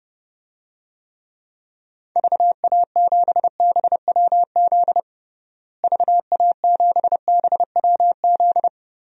Morse code, heard as a keyed tone, about 32 wpm, 700 Hz. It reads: VA7BWZ VA7BWZ
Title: DXCC-100 Call Signs - 6 Characters - Repeated 2x - 30wpm